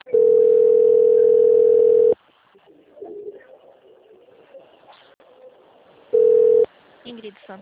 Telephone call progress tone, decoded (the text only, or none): ringback tone